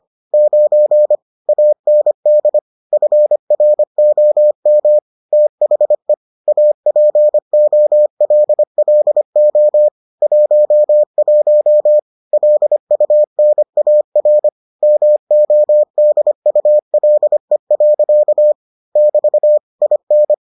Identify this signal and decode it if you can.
Morse code, heard as a keyed tone, about 25 wpm, 600 Hz. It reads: 9 AND FROM THE APOLLO 11 LUNAR MODULE. <BT> IN